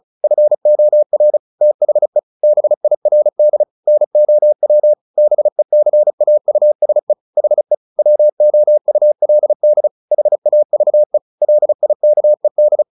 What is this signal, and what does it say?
Morse code, heard as a keyed tone, about 34 wpm, 600 Hz. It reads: FOR THE BIRD NOW BECAUSE HE WOULD HAVE LIKED
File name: the old man and the sea / the old man and the sea 1 35wpm